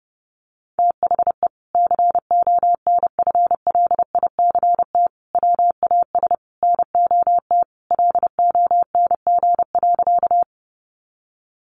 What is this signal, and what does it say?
Morse code, heard as a keyed tone, about 30 wpm, 700 Hz. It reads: THE CONFLICT WAS NOT LONG.
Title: Sentences from Top 1300 Words 30wpm